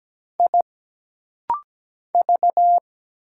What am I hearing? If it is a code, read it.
Morse code, heard as a keyed tone, about 17 wpm, 700 Hz. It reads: I V
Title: Single Letter-Number - Rapid-Fire 17wpm